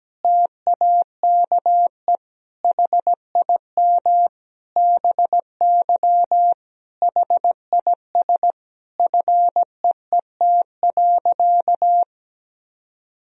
Morse code, heard as a keyed tone, about 17 wpm, 700 Hz. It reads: TAKE HIM BY HIS FEET.
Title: Sentences from Top 1000 Words - Review Part 1 - 17wpm